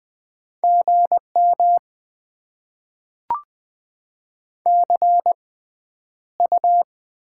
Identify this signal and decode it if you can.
Morse code, heard as a keyed tone, about 20 wpm, spaced as for 10 wpm, 700 Hz. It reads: GM CU
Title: Top 20 QSO Elements - 6x Character Spacing - 20wpm